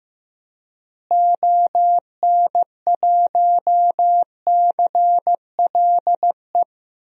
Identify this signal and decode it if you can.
Morse code, heard as a keyed tone, about 16 wpm, 700 Hz. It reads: ON1CLE
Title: DXCC-100 Call Signs - 6 Characters - Repeated 2x - 15wpm